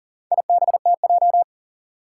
Morse code, heard as a keyed tone, about 43 wpm, 700 Hz. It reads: IBTJ
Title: Four Challenging Letter-Number Combinations 40wpm